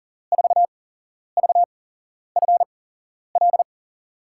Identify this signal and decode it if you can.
Morse code, heard as a keyed tone, about 40 wpm, spaced as for 11 wpm, 700 Hz. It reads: 4VFL
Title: USA Call Signs - 5 Characters - 8x Character Spacing - 40wpm